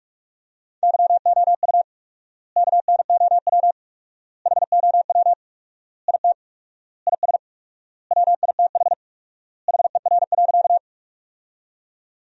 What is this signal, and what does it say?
Morse code, heard as a keyed tone, about 46 wpm, 700 Hz. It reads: YOU KNOW HOW IT IS WITH HER.
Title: 4x Word-Spacing - Sentences from Top 100 Words - 45wpm